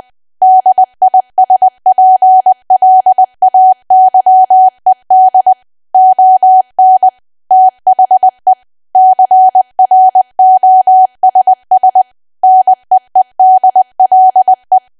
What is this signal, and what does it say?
Morse code, heard as a keyed tone, about 20 wpm, 750 Hz. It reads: DISPLAYED ON THE CROSS NEEDLE
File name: cwa / W1AW-Level3 / 20WPM